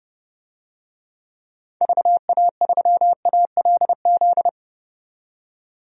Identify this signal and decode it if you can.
Morse code, heard as a keyed tone, about 30 wpm, 700 Hz. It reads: VA3ALZ
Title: DXCC-100 Call Signs - 6 Characters - 30wpm